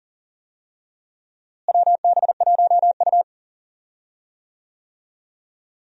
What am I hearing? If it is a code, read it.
Morse code, heard as a keyed tone, about 40 wpm, 700 Hz. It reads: WB1U
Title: USA Call Signs - 4 Characters - Repeated 3x - 40wpm